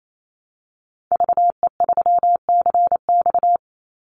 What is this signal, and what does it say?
Morse code, heard as a keyed tone, about 28 wpm, 700 Hz. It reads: VE3CX
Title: Call Signs Repeated 2x - 28wpm